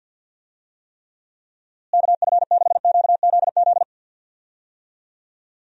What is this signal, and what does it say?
Morse code, heard as a keyed tone, about 49 wpm, 700 Hz. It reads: KF6QZZ
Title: USA Call Signs - 6 Characters - Repeated 2x - 50wpm